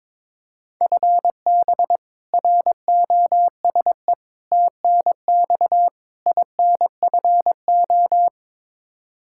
Morse code, heard as a keyed tone, about 22 wpm, 700 Hz. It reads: FB ROSE TNX INFO
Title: Practice QSO - Exchange 3 - More Info - 22wpm